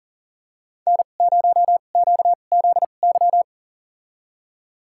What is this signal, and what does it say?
Morse code, heard as a keyed tone, about 40 wpm, spaced as for 29 wpm, 700 Hz. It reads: N0QZY
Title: USA Call Signs - 5 Characters - 2x Character Spacing - 40wpm